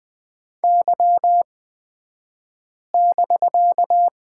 Morse code, heard as a keyed tone, about 20 wpm, spaced as for 5 wpm, 700 Hz. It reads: Y<BK>
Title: 249 Lesson 40d - BK - Two Characters 20wpm